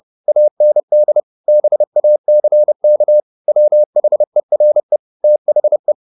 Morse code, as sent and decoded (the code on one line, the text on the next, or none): .- -. -.. / -... .- -.-. -.- / .-- .... .-. / - .... .
AND BACK WHERE THE